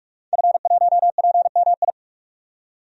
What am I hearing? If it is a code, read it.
Morse code, heard as a keyed tone, about 42 wpm, 700 Hz. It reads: F1PMI
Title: DXCC-100 Call Signs - 5 Characters - 45wpm